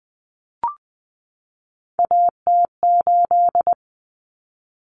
Morse code, heard as a keyed tone, about 20 wpm, 700 Hz. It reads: AT8
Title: Common Cut Numbers - 3-Digit 20wpm